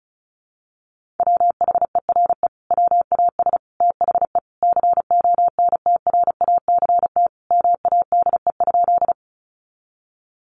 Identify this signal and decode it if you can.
Morse code, heard as a keyed tone, about 34 wpm, 700 Hz. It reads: WHERE WAS THE CONTRACT MADE?
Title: Sentences from Top 1400 Words 35wpm